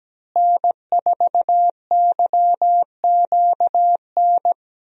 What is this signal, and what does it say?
Morse code, heard as a keyed tone, about 17 wpm, 700 Hz. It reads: N4YQN
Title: USA Call Signs - 5 Characters - Repeated 3x - 17wpm